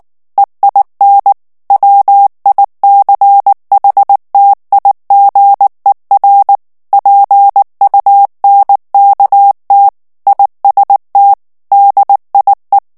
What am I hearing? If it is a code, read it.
Morse code, heard as a keyed tone, about 19 wpm, 800 Hz. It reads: EIN WICHTIGER PUNKT IST DIE
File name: morse mp3 / ALLEMAND texte clair 2 vitesse 18 mots minute